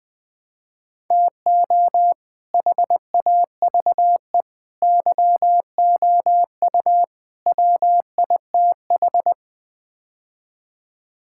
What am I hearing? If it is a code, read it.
Morse code, heard as a keyed tone, about 21 wpm, 700 Hz. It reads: TO HAVE YOU WITH
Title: Build A Phrase - 5-Word Phrases - 20wpm